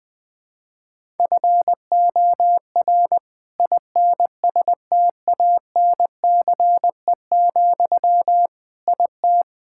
Morse code, heard as a keyed tone, about 20 wpm, 700 Hz. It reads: FOR INSTANCE, IT
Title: Sentences from Top 1000 Words - Review Part 5 - 20wpm